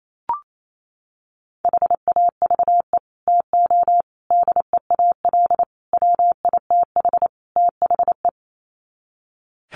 Morse code, heard as a keyed tone, about 28 wpm, 700 Hz. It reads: HAVE TO DEAL WITH THE
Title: Sets of 5 Words Encore2 28wpm